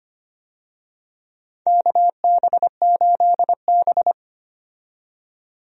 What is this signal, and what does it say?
Morse code, heard as a keyed tone, about 25 wpm, 700 Hz. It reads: KB8B
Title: USA Call Signs - 4 Characters - 25wpm